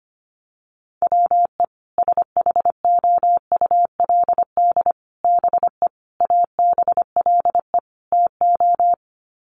Morse code, heard as a keyed tone, about 25 wpm, 700 Hz. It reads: WE SHOULD BE ABLE TO
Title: Sets of 5 Words Encore2 25wpm